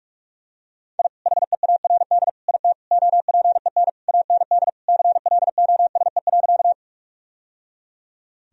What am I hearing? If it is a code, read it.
Morse code, heard as a keyed tone, about 42 wpm, 700 Hz. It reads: I HEARD IT OPEN AND CLOSE.
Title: Sentences from Top 1000 Words - Review Part 1 - 45wpm